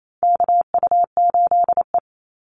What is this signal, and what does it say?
Morse code, heard as a keyed tone, about 28 wpm, 700 Hz. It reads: KU8E
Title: Call Signs Repeated 3x - 28wpm